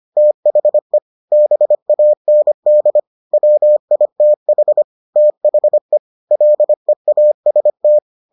Morse code, heard as a keyed tone, about 25 wpm, 600 Hz. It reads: THE BAND WITH THE LEAST